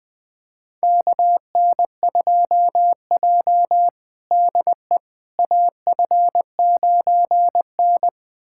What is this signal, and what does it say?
Morse code, heard as a keyed tone, about 20 wpm, 700 Hz. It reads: KN2J DE AF9N
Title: USA Callsign DE Callsign - 4 Characters - 20wpm